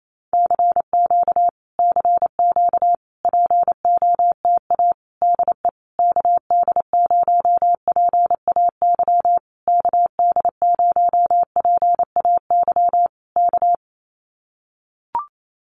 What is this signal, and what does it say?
Morse code, heard as a keyed tone, about 28 wpm, 700 Hz. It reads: CQ CQ POTA DE KD0PAY KD0PAY K
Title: Common QSO Phrases Encore 28wpm